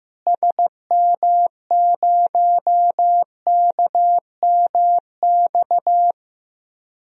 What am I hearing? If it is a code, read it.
Morse code, heard as a keyed tone, about 16 wpm, 700 Hz. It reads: SM0KMX
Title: DXCC-100 Call Signs - 6 Characters - Repeated 2x - 15wpm